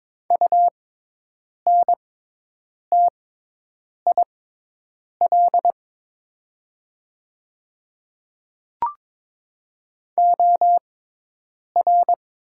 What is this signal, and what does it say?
Morse code, heard as a keyed tone, about 22 wpm, spaced as for 8 wpm, 700 Hz. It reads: UNTIL OR
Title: 6x Character Spacing - Top 500 Words - 22wpm